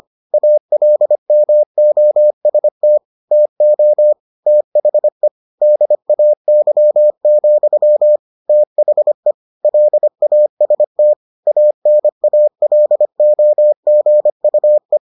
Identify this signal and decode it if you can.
Morse code, heard as a keyed tone, about 25 wpm, 600 Hz. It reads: ALMOST TO THE DAY, THE LAST ANALOGUE